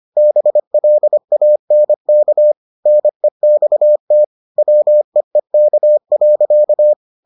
Morse code, heard as a keyed tone, about 25 wpm, 600 Hz. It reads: BLANK NEXT WEEK.